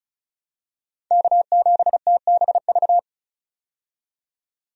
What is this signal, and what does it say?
Morse code, heard as a keyed tone, about 34 wpm, 700 Hz. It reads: K7TBV
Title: USA Call Signs - 5 Characters - 35wpm